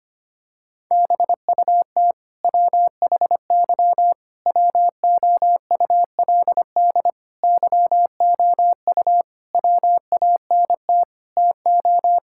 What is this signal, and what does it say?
Morse code, heard as a keyed tone, about 25 wpm, 700 Hz. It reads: BUT WHY WOULD YOU WANT TO
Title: Sentences from Top 1000 Words - Review Part 6 - 25wpm